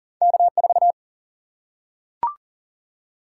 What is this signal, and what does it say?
Morse code, heard as a keyed tone, about 39 wpm, 700 Hz. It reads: K4
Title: Single-Letter Single-Number 40wpm